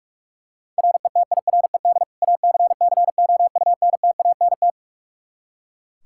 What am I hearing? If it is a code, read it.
Morse code, heard as a keyed tone, about 43 wpm, 700 Hz. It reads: RETIRED ACCOUNTANT
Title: K1USN SST QSO Practice 45wpm